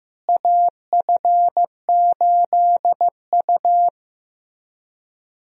Morse code, heard as a keyed tone, about 15 wpm, 700 Hz. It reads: AF8U
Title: USA Call Signs - 4 Characters - Repeated 2x - 15wpm